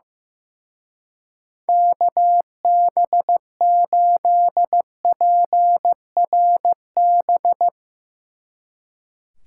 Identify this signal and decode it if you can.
Morse code, heard as a keyed tone, about 15 wpm, 700 Hz. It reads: KB8PRB
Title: Speed Racing - USA 2x3 Call Signs 15wpm